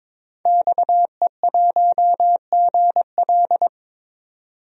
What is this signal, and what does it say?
Morse code, heard as a keyed tone, about 21 wpm, 700 Hz. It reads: XE1GL
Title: DXCC-100 Call Signs - 5 Characters - 22wpm